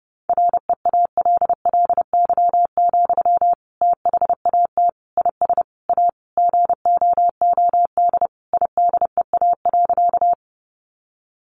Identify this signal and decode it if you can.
Morse code, heard as a keyed tone, about 30 wpm, 700 Hz. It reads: REALLY, THAT IS A GOOD IDEA.
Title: Sentences from Top 600 Words 30wpm